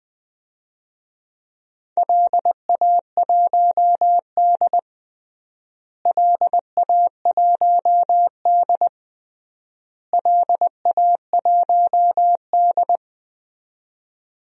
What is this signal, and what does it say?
Morse code, heard as a keyed tone, about 20 wpm, 700 Hz. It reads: LA1D LA1D LA1D
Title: DXCC-100 Call Signs - 4 Characters - Repeated 3x - 20wpm